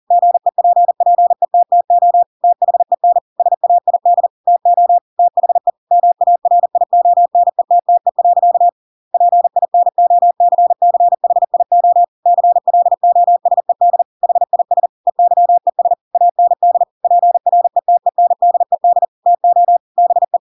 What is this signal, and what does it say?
Morse code, heard as a keyed tone, about 40 wpm, 700 Hz. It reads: GEPPETTO THEN SAID TO THE MARIONETTE. PINOCCHIO CLOSED HIS EYES AND PRETENDED TO BE
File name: Pinocchio / Pinocchio ch8 40 40 ik1hge